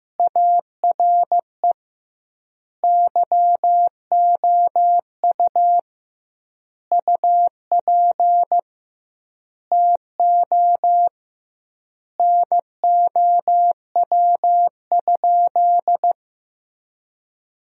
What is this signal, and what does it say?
Morse code, heard as a keyed tone, about 15 wpm, 700 Hz. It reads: ARE YOU UP TO NOW?
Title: 2x Word-Spacing - Sentences from Top 100 Words - 15wpm